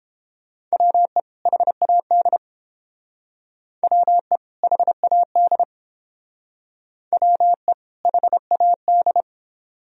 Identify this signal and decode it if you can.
Morse code, heard as a keyed tone, about 30 wpm, 700 Hz. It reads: WE HAD WE HAD WE HAD